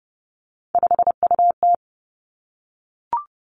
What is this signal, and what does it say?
Morse code, heard as a keyed tone, about 30 wpm, 700 Hz. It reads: 5UT
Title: Three Challenging Letter-Number Combinations 30wpm